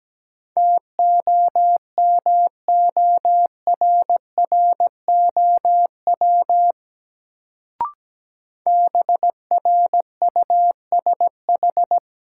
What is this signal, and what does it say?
Morse code, heard as a keyed tone, about 17 wpm, 700 Hz. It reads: TOMORROW BRUSH
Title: Sentences from Top 1100 Words - New Word Drill Alt - 17wpm